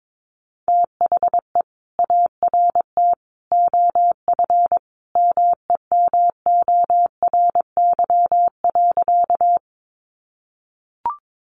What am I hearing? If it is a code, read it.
Morse code, heard as a keyed tone, about 22 wpm, 700 Hz. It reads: THE ART OF MEMORY.
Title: Sentences from Top 1300 Words 22wpm